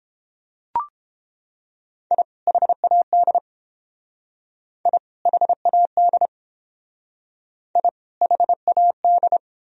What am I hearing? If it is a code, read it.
Morse code, heard as a keyed tone, about 29 wpm, 700 Hz. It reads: I HAD I HAD I HAD